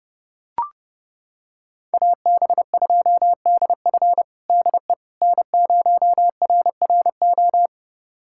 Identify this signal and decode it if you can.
Morse code, heard as a keyed tone, about 32 wpm, 700 Hz. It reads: AB2DF DE N0RRO
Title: USA Callsign DE Callsign 30wpm